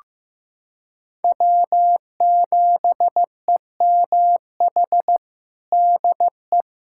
Morse code, heard as a keyed tone, about 15 wpm, 700 Hz. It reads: W7EMH DE
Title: USA Callsign DE Callsign - 5 Characters - 15wpm